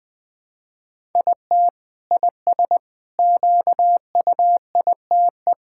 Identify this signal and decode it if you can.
Morse code, heard as a keyed tone, about 20 wpm, 700 Hz. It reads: IT IS QUITE
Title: Sentences from Top 1000 Words - Review Part 5 - 20wpm